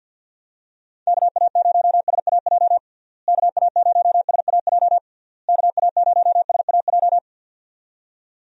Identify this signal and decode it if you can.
Morse code, heard as a keyed tone, about 51 wpm, 700 Hz. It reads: KA0SAJ KA0SAJ KA0SAJ
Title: USA Call Signs - 6 Characters - Repeated 3x - 50wpm